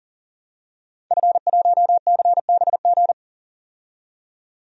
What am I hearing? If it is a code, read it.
Morse code, heard as a keyed tone, about 40 wpm, 700 Hz. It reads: F1CBG
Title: DXCC-100 Call Signs - 5 Characters - 40wpm